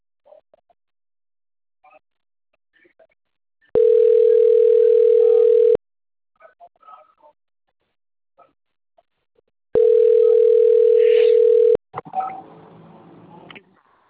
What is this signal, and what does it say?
Ringback tone (North America), 440 + 480 Hz.
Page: Atibaia 1 CP Cartao